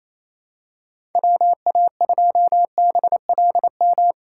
Morse code, heard as a keyed tone, about 28 wpm, 700 Hz. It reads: WA2BLM
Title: USA Callsign DE Callsign 28wpm